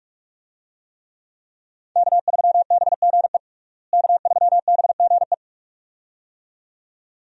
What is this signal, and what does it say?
Morse code, heard as a keyed tone, about 46 wpm, 700 Hz. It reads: K3BGE K3BGE
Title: USA Call Signs - 5 Characters - Repeated 2x - 45wpm